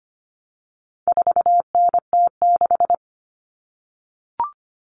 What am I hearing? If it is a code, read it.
Morse code, heard as a keyed tone, about 25 wpm, 700 Hz. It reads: 4NT6
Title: Common Cut Numbers - 4-Digit 25wpm